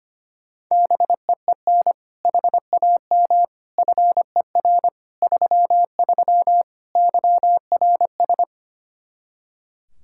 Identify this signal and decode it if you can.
Morse code, heard as a keyed tone, about 25 wpm, 700 Hz. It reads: BEEN HAM FER 33 YRS
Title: Practice QSO - Exchange 3 - More Info - 25wpm